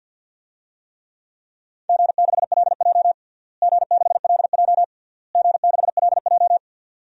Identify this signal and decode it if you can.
Morse code, heard as a keyed tone, about 53 wpm, 700 Hz. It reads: G6LJ G6LJ G6LJ
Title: DXCC-100 Call Signs - 4 Characters - Repeated 3x - 50wpm